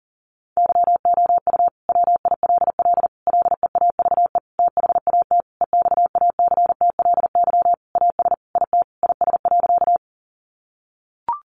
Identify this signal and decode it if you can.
Morse code, heard as a keyed tone, about 40 wpm, 700 Hz. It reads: YOU WILL LEAVE THAT EXACTLY AS IT IS.
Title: Sentences from Top 900 Words 40wpm